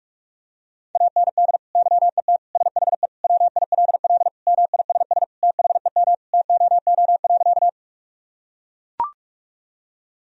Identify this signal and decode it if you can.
Morse code, heard as a keyed tone, about 43 wpm, 700 Hz. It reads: AND YET SHE WILL MISS THEM TOO.
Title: Sentences from Top 1000 Words - Review Part 3 - 45wpm